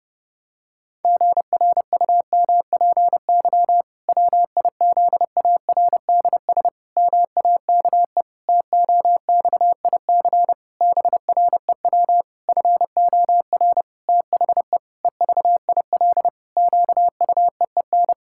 Morse code, heard as a keyed tone, about 30 wpm, 700 Hz. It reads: GRUMPY WIZARDS MAKE TOXIC BREW FOR THE EVIL QUEEN
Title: Pangram Challenge v2 - 30wpm